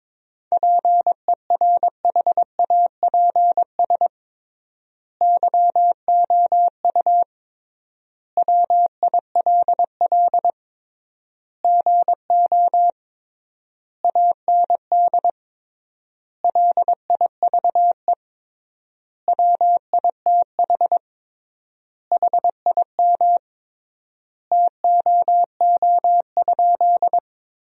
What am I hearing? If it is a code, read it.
Morse code, heard as a keyed tone, about 22 wpm, 700 Hz. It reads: PERHAPS YOU WILL GO AND LIVE WITH HIM TOO?
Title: 3x Word-Spacing - Sentences from Top 500 Words - 22wpm